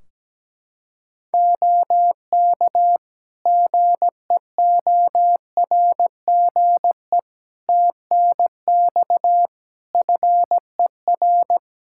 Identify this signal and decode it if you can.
Morse code, heard as a keyed tone, about 17 wpm, 700 Hz. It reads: OK GEORGE TNX FER
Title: K1USN SST QSO Practice 17wpm